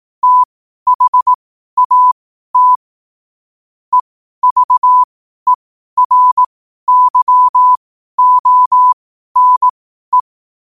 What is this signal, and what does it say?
Morse code, heard as a keyed tone, about 18 wpm, spaced as for 12 wpm, 1000 Hz. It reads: THAT EVERYONE
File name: qotdfiles / 12wpm